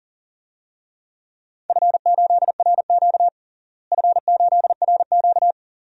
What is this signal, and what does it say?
Morse code, heard as a keyed tone, about 40 wpm, 700 Hz. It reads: F8RQ F8RQ
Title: DXCC-100 Call Signs - 4 Characters - Repeated 3x - 40wpm